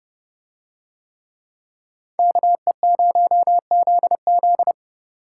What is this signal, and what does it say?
Morse code, heard as a keyed tone, about 30 wpm, 700 Hz. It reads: KE0ZZ